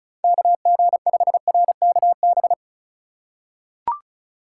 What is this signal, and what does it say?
Morse code, heard as a keyed tone, about 36 wpm, 700 Hz. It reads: KG5RKB
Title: USA Call Signs - 6 Characters - 2x Character Spacing - 35wpm